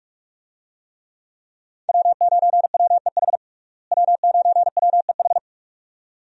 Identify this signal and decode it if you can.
Morse code, heard as a keyed tone, about 46 wpm, 700 Hz. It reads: W9WEH W9WEH
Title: USA Call Signs - 5 Characters - Repeated 2x - 45wpm